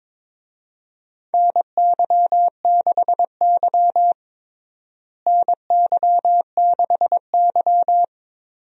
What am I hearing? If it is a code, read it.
Morse code, heard as a keyed tone, about 22 wpm, 700 Hz. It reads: NY6Y NY6Y
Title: USA Call Signs - 4 Characters - Repeated 3x - 22wpm